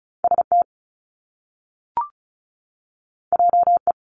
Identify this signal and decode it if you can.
Morse code, heard as a keyed tone, about 36 wpm, 700 Hz. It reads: ST JE
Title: Top Two Letter Combinations in Words 35wpm